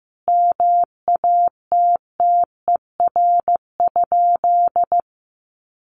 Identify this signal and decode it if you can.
Morse code, heard as a keyed tone, about 15 wpm, 700 Hz. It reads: MATTER?
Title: Sentences from Top 700 Words 15wpm